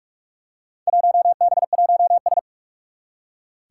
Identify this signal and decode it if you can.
Morse code, heard as a keyed tone, about 43 wpm, 700 Hz. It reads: 1B1S
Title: Four Challenging Letter-Number Combinations 45wpm